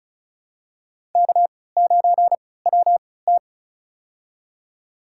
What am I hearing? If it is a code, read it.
Morse code, heard as a keyed tone, about 34 wpm, spaced as for 20 wpm, 700 Hz. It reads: K9WT
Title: USA Call Signs - 4 Characters - 3x Character Spacing - 35wpm